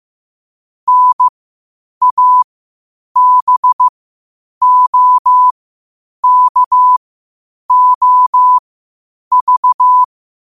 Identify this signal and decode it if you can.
Morse code, heard as a keyed tone, about 15 wpm, spaced as for 8 wpm, 1000 Hz. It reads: NABOKOV
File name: qotdfiles / 8wpm